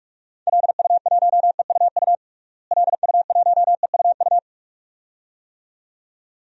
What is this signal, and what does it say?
Morse code, heard as a keyed tone, about 41 wpm, 700 Hz. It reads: LU1EUU LU1EUU
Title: DXCC-100 Call Signs - 6 Characters - Repeated 2x - 45wpm